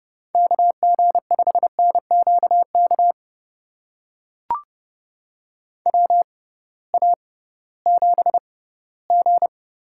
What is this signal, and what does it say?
Morse code, heard as a keyed tone, about 31 wpm, spaced as for 18 wpm, 700 Hz. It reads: KG5NQK WA7G
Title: USA Call Signs - 6 Characters - 6x Character Spacing - 30wpm